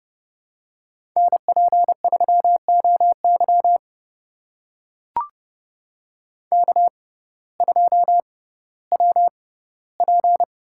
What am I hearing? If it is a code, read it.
Morse code, heard as a keyed tone, about 31 wpm, spaced as for 17 wpm, 700 Hz. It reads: NP3OY K2WP
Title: USA Call Signs - 6x Character Spacing - 30wpm